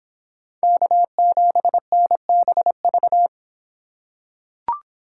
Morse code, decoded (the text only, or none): K7NBV